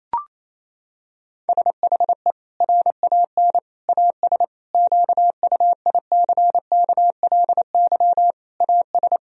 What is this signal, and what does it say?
Morse code, heard as a keyed tone, about 28 wpm, 700 Hz. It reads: SHE RAN AS QUICKLY AS
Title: Sentences from Top 1000 Words - Review Part 4 - 28wpm